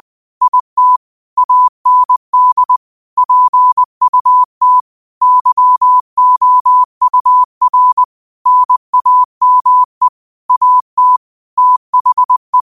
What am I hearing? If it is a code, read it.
Morse code, heard as a keyed tone, about 20 wpm, 1000 Hz. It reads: IT AND PUT YOUR NAME AT THE